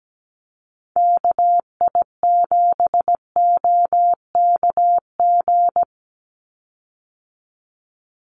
Morse code, heard as a keyed tone, about 17 wpm, 700 Hz. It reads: KI7OKG
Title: USA Call Signs 17wpm